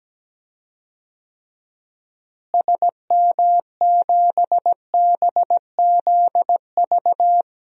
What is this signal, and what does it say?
Morse code, heard as a keyed tone, about 17 wpm, 700 Hz. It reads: SM7BZV
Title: DXCC-100 Call Signs - 6 Characters - Repeated 3x - 17wpm